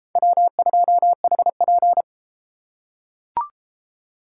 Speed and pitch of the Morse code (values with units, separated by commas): 32 wpm, 700 Hz